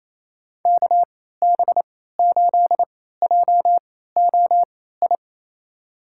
Morse code, heard as a keyed tone, about 28 wpm, spaced as for 16 wpm, 700 Hz. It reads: KB8JOI
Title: USA Call Signs - 6 Characters - 3x Character Spacing - 28wpm